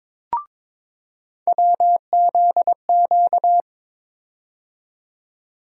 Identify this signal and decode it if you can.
Morse code, heard as a keyed tone, about 22 wpm, 700 Hz. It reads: WZQ
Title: USA Call Sign Postfixes - 3 Letters 22wpm